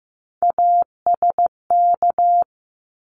Morse code, heard as a keyed tone, about 15 wpm, 700 Hz. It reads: ASK
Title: ED Suffix - Top 100 Words with Base Words 15wpm